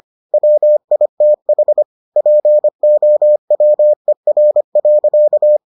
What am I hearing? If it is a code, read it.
Morse code, heard as a keyed tone, about 25 wpm, 600 Hz. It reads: WITH POWER.